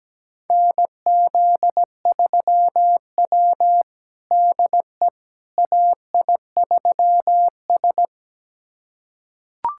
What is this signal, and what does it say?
Morse code, heard as a keyed tone, about 17 wpm, 700 Hz. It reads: NZ3W DE AI3S
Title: USA Callsign DE Callsign - 4 Characters - 17wpm